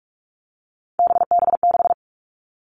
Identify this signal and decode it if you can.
Morse code, heard as a keyed tone, about 46 wpm, 700 Hz. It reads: BB6
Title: Three Challenging Letter-Number Combinations 45wpm